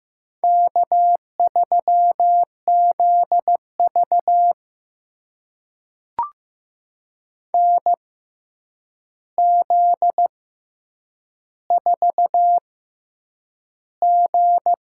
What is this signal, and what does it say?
Morse code, heard as a keyed tone, about 15 wpm, spaced as for 8 wpm, 700 Hz. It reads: K3ZV NZ4G
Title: USA Call Signs - 4 Characters - 6x Character Spacing - 15wpm